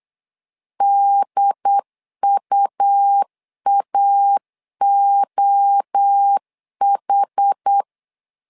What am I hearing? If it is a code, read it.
Morse code, heard as a keyed tone, about 8 wpm, 800 Hz. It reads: DUAOH
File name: LEZIONI / 4 Q 8wpm 2732